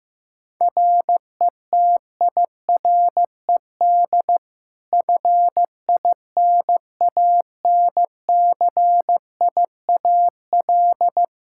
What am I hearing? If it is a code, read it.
Morse code, heard as a keyed tone, about 16 wpm, 700 Hz. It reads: RETIRED FINANCIAL
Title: K1USN SST QSO Practice 15wpm